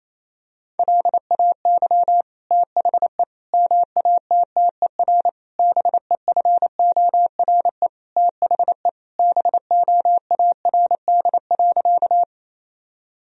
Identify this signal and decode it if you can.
Morse code, heard as a keyed tone, about 28 wpm, 700 Hz. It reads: LAY THE MATTER BEFORE THE BOARD.
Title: Sentences from Top 1000 Words - Review Part 4 - 28wpm